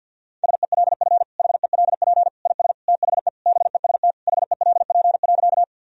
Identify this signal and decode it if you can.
Morse code, heard as a keyed tone, about 52 wpm, 700 Hz. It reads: SELF HELP IS THE BEST HELP.
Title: Sentences from Top 1000 Words - Review Part 1 - 50wpm